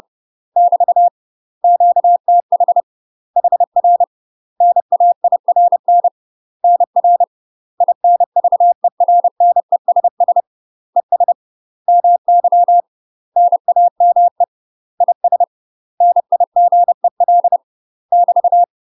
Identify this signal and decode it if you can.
Morse code, heard as a keyed tone, about 30 wpm, 700 Hz. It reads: <BT> QTH HR NAIRN NR INVERNESS ES MY NAME IS NIGEL <BT>
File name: UK MASTER QSOs TRAINING / 30WPM / UK MASTER QSOs 0014